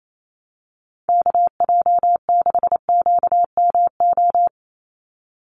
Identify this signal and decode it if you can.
Morse code, heard as a keyed tone, about 28 wpm, 700 Hz. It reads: KJ6QMO
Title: USA Call Signs 28wpm